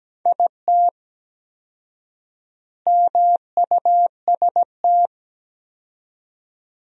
Morse code, heard as a keyed tone, about 17 wpm, 700 Hz. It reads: IT MUST
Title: 4x Word-Spacing - Sentences from Top 500 Words - 17wpm